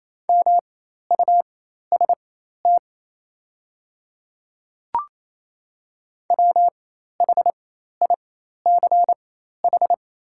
Morse code, heard as a keyed tone, about 28 wpm, spaced as for 13 wpm, 700 Hz. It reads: MUST WHICH
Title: 4x Character Spacing - Top 200 Words - 28wpm